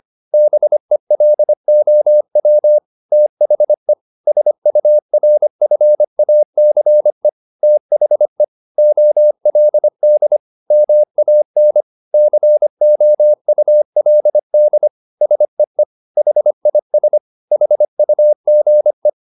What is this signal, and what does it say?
Morse code, heard as a keyed tone, about 25 wpm, 600 Hz. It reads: BELOW THE SURFACE THE OLD MAN COULD SEE HIS HUGE